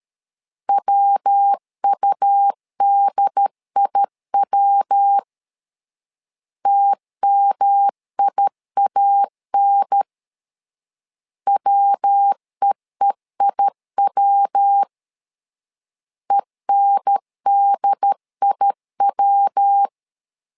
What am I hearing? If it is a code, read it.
Morse code, heard as a keyed tone, about 13 wpm, 800 Hz. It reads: WUDIW TMIAN WEEIW ENDIW
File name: LEZIONI / 3 MWRO / 3 MWRO 12wpm 1730